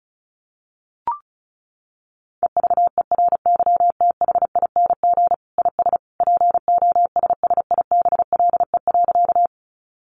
Morse code, heard as a keyed tone, about 34 wpm, 700 Hz. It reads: EVERYTHING IS POSSIBLE.
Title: Sentences from Top 700 Words 35wpm